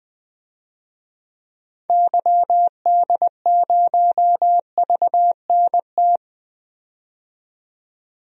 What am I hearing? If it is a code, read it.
Morse code, heard as a keyed tone, about 20 wpm, 700 Hz. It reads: YD0VNT
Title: DXCC-100 Call Signs - 6 Characters - 20wpm